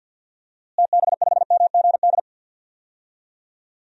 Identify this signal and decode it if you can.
Morse code, heard as a keyed tone, about 50 wpm, 700 Hz. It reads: TB5MGD
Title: DXCC-100 Call Signs - 6 Characters - Repeated 2x - 50wpm